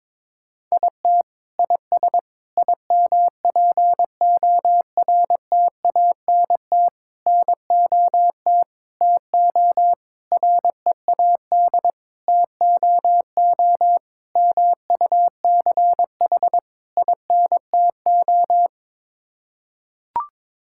Morse code, heard as a keyed tone, about 22 wpm, 700 Hz. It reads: IT IS IMPORTANT NOT TO READ TOO MUCH INTO
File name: Sets of 9 Words - 22wpm